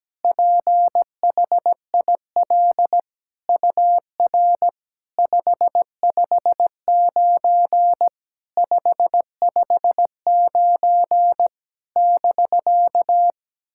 Morse code, heard as a keyed tone, about 17 wpm, 700 Hz. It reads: PHIL UR 559 559 <BK>
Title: SOTA QSO Practice 17wpm